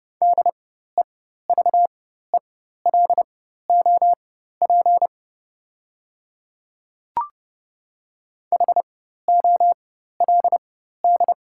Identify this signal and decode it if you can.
Morse code, heard as a keyed tone, about 30 wpm, spaced as for 14 wpm, 700 Hz. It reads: DEVELOP HOLD
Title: 4x Character Spacing - Top 500 Words - 30wpm